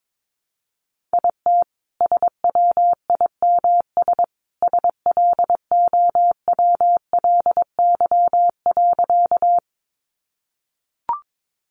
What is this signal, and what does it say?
Morse code, heard as a keyed tone, about 22 wpm, 700 Hz. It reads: IT SWIMS SLOWLY.
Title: Easy Phrases 22wpm